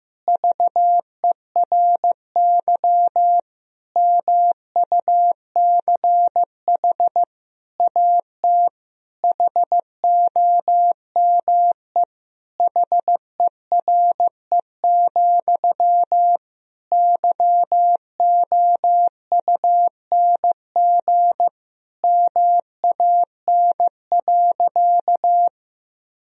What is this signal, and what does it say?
Morse code, heard as a keyed tone, about 15 wpm, 700 Hz. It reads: VERY MUCH AT HOME HERE, YOUNG MAN.
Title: Sentences from Top 1000 Words - Review Part 1 - 15wpm